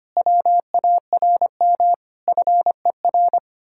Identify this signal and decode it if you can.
Morse code, heard as a keyed tone, about 29 wpm, 700 Hz. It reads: WARM FER
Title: Speed-Racing Common QSO Phrases Encore - 25wpm